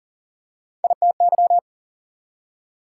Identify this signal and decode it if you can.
Morse code, heard as a keyed tone, about 40 wpm, 700 Hz. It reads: ITY
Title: Prefixes and Suffixes 40wpm